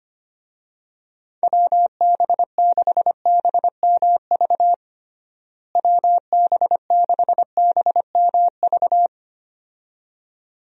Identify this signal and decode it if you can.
Morse code, heard as a keyed tone, about 25 wpm, 700 Hz. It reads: WB6BMV WB6BMV
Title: USA Call Signs - 6 Characters - Repeated 2x - 25wpm